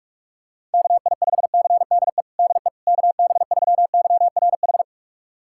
Morse code, heard as a keyed tone, about 46 wpm, 700 Hz. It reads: KI5CDE DE KB3YRH
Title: USA Callsign DE Callsign 45wpm